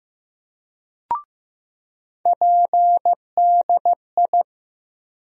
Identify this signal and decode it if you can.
Morse code, heard as a keyed tone, about 15 wpm, 700 Hz. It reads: PDI
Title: USA Call Sign Postfixes - 3 Letters 15wpm